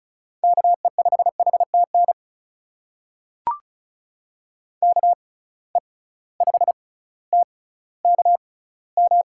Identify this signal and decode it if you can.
Morse code, heard as a keyed tone, about 36 wpm, spaced as for 18 wpm, 700 Hz. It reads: KE5HTN KE5TKM
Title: USA Call Signs - 6 Characters - 6x Character Spacing - 35wpm